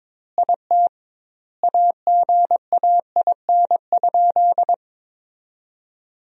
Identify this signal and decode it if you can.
Morse code, heard as a keyed tone, about 22 wpm, 700 Hz. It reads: IT AGAIN?
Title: 2x Word-Spacing - Sentences from Top 200 Words - 22wpm